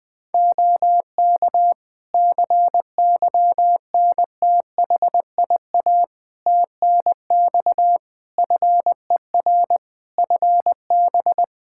Morse code, heard as a keyed tone, about 20 wpm, 700 Hz. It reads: OK CYNTHIA TNX FER FB
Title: K1USN SST QSO Practice 20wpm